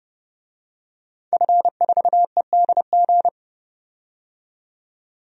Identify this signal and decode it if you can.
Morse code, heard as a keyed tone, about 30 wpm, 700 Hz. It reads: F4EDG
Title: DXCC-100 Call Signs - 5 Characters - 30wpm